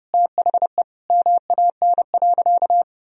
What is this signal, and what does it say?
Morse code, heard as a keyed tone, about 32 wpm, 700 Hz. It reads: THE MAN.
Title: Sentences from Top 200 Words and Abbreviations 30wpm